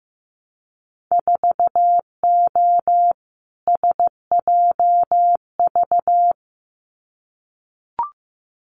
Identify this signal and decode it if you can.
Morse code, heard as a keyed tone, about 15 wpm, 700 Hz. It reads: 4O SJV
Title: Winter Field Day Exchanges Encore 15wpm